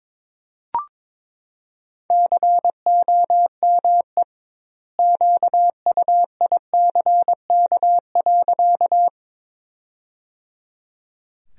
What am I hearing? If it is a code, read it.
Morse code, heard as a keyed tone, about 22 wpm, 700 Hz. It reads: COME QUICK.
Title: 2x Word-Spacing - Sentences from Top 500 Words - 22wpm